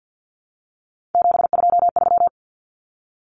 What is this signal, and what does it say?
Morse code, heard as a keyed tone, about 50 wpm, 700 Hz. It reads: 723
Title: Common Cut Numbers - 3-Digit 50wpm